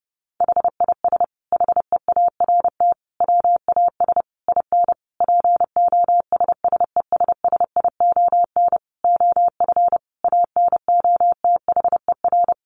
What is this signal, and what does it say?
Morse code, heard as a keyed tone, about 30 wpm, 700 Hz. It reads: HIS HEART WAS IN POSSESSION OF ANOTHER
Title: Sentences from Top 1400 Words 30wpm